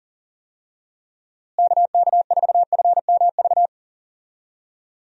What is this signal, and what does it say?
Morse code, heard as a keyed tone, about 40 wpm, 700 Hz. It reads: KK4FMV
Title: USA Call Signs - 6 Characters - Repeated 2x - 40wpm